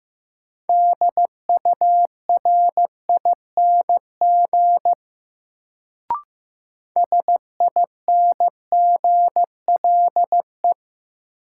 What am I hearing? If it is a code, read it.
Morse code, heard as a keyed tone, about 15 wpm, 700 Hz. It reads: DURING SINGLE
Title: Sentences from Top 700 Words - New Word Drill - 15wpm